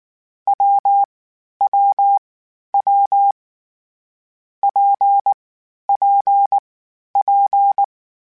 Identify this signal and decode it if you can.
Morse code, heard as a keyed tone, about 19 wpm, 800 Hz. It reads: W W W P P P
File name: Variable Speed / Variable Speed-mp3s / WPJ WPJ1 sp @18